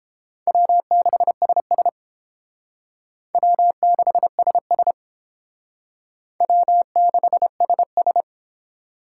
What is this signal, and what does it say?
Morse code, heard as a keyed tone, about 31 wpm, 700 Hz. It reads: W6SS W6SS W6SS